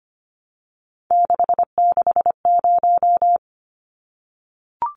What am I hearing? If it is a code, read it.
Morse code, heard as a keyed tone, about 25 wpm, 690 Hz. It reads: 660
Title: Numbers - 3-Digit 25wpm